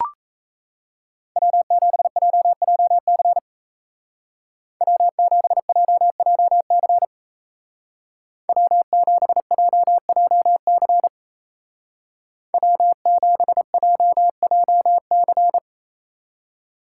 Morse code, heard as a keyed tone, about 35 wpm, 700 Hz. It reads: W7JJC W7JJC W7JJC W7JJC
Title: Speed Racing - USA 1x3 Call Signs 28wpm